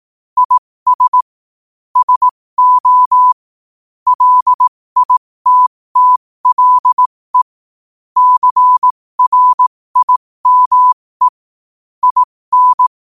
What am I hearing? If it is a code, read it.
Morse code, heard as a keyed tone, about 18 wpm, spaced as for 15 wpm, 1000 Hz. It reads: IS SO LITTLE CRIME IN
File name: qotdfiles / 15wpm